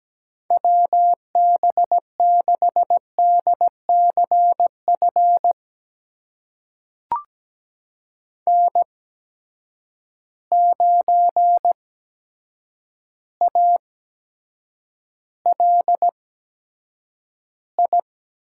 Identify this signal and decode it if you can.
Morse code, heard as a keyed tone, about 17 wpm, spaced as for 8 wpm, 700 Hz. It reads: WB6DCF N9ALI
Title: USA Call Signs - 8x Character Spacing - 17wpm